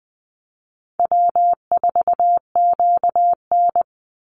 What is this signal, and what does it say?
Morse code, heard as a keyed tone, about 20 wpm, 700 Hz. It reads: W4QN
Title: Easy Call Signs - 20wpm